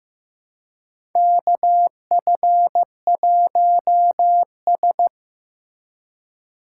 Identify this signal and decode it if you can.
Morse code, heard as a keyed tone, about 15 wpm, 700 Hz. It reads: KF1S
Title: USA Call Signs - 4 Characters - 4x Character Spacing - 15wpm